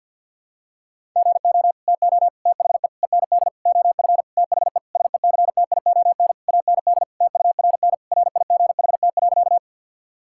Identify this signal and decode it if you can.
Morse code, heard as a keyed tone, about 53 wpm, 700 Hz. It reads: GO TO THE END OF THE SECTION AND TURN RIGHT.
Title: Sentences from Top 1000 Words - Review Part 5 - 50wpm